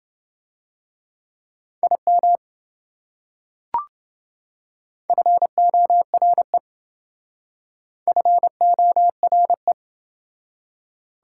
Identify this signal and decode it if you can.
Morse code, heard as a keyed tone, about 28 wpm, 700 Hz. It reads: IM FORE FORE